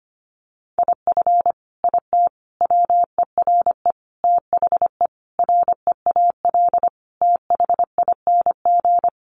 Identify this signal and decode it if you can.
Morse code, heard as a keyed tone, about 25 wpm, 700 Hz. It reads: IF IT WERE THE REAL THING
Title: Sentences from Top 1300 Words 25wpm